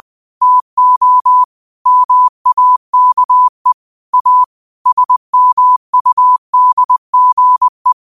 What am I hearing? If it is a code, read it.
Morse code, heard as a keyed tone, about 20 wpm, 1000 Hz. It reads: TO MAKE A SMUDGE